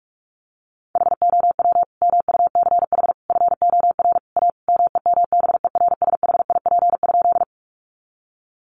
Morse code, heard as a keyed tone, about 44 wpm, 700 Hz. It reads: HOW MUCH FOR A MEMBERSHIP?
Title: Sentences from Top 1400 Words 45wpm